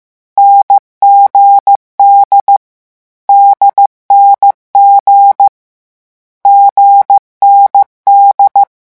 Morse code, heard as a keyed tone, about 15 wpm, 800 Hz. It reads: NGD DNG GND
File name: morse mp3 / lecon4-vitesse 14